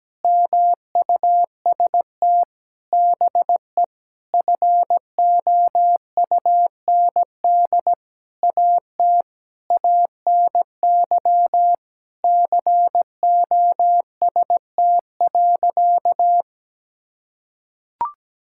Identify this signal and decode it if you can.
Morse code, heard as a keyed tone, about 17 wpm, 700 Hz. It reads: MUST BE FOUND AT ANY COST.
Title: Sentences from Top 1000 Words - Review Part 5 - 17wpm